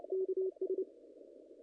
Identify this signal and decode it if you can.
Morse code, heard as a keyed tone, about 28 wpm, 410 Hz. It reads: KS
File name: contests / ss-cw-k5tr-op-n5rz -0100